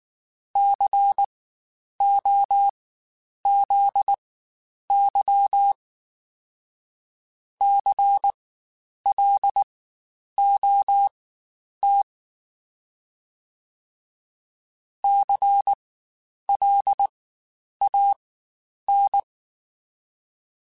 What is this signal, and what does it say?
Morse code, heard as a keyed tone, about 19 wpm, spaced as for 9 wpm, 800 Hz. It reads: COZY CLOT CLAN
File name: Spaced Alphas / Spaced Alphas mp3s / Cs sp @18